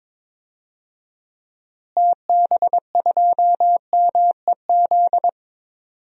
Morse code, heard as a keyed tone, about 22 wpm, 700 Hz. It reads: TB2MEZ
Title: DXCC-100 Call Signs - 6 Characters - Repeated 3x - 22wpm